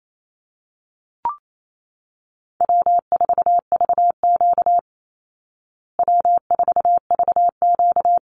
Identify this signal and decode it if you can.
Morse code, heard as a keyed tone, about 28 wpm, 700 Hz. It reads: W4VQ W4VQ
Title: Easy Call Signs Repeated 2x - 28wpm